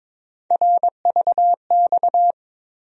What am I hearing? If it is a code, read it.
Morse code, heard as a keyed tone, about 22 wpm, 700 Hz. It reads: RVX
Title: USA Call Sign Postfixes - 3 Letters 22wpm